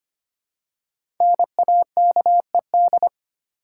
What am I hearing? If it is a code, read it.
Morse code, heard as a keyed tone, about 25 wpm, 700 Hz. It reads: NAKED
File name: 5 Letter Words - First 3 Letters Given - 25wpm